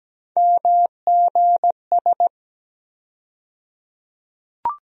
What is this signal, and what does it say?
Morse code, heard as a keyed tone, about 17 wpm, 700 Hz. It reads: MGS
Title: USA Call Sign Postfixes - 3 Letters 17wpm